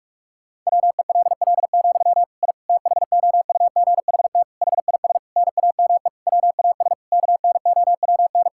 Morse code, heard as a keyed tone, about 46 wpm, 700 Hz. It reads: WELL, I THOUGHT HIS NAME WAS KNOWN
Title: Sentences from Top 1000 Words - Review Part 5 - 45wpm